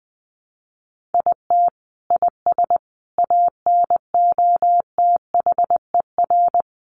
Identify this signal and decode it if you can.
Morse code, heard as a keyed tone, about 20 wpm, 700 Hz. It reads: IT IS ANOTHER
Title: Sentences from Top 1100 Words 20wpm